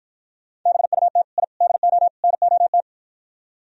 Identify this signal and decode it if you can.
Morse code, heard as a keyed tone, about 53 wpm, 700 Hz. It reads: BUT I DO NOT